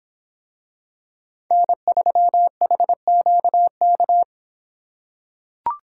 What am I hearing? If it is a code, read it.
Morse code, heard as a keyed tone, about 25 wpm, 700 Hz. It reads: N3HQK